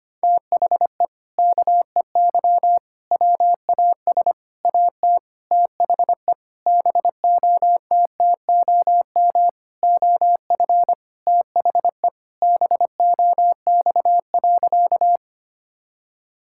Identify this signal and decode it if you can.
Morse code, heard as a keyed tone, about 25 wpm, 700 Hz. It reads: THE KEY WAS AT THE BOTTOM OF THE BOX.
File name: practice / 25wpm / Sentences with 4-Word Phrases - 25wpm